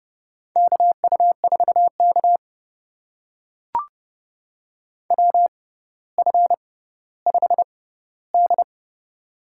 Morse code, heard as a keyed tone, about 30 wpm, spaced as for 15 wpm, 700 Hz. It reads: KU4K WF5D
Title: USA Call Signs - 4 Characters - 6x Character Spacing - 30wpm